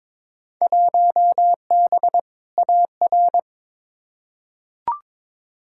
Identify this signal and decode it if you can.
Morse code, heard as a keyed tone, about 22 wpm, 700 Hz. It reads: 1B AR
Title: Field Day Exchanges Encore - 22wpm